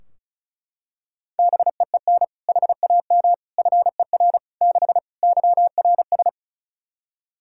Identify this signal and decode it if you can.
Morse code, heard as a keyed tone, about 36 wpm, 700 Hz. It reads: BEEN HAM FER 6 YRS
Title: K1USN SST QSO Practice 35wpm